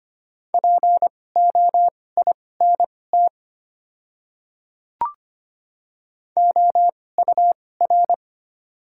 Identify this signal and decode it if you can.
Morse code, heard as a keyed tone, about 25 wpm, spaced as for 18 wpm, 700 Hz. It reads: POINT OUR
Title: 2x Character Spacing - Top 200 Words - 25wpm